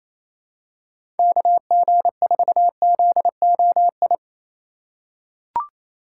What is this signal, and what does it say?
Morse code, heard as a keyed tone, about 28 wpm, 700 Hz. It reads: KG4ZOI
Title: USA Call Signs - 6 Characters - 3x Character Spacing - 28wpm